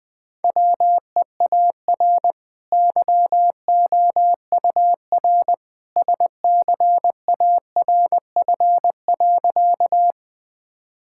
Morse code, heard as a keyed tone, about 20 wpm, 700 Hz. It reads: WEAR YOUR SCARF.
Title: Easy Sentences with Hints - 20wpm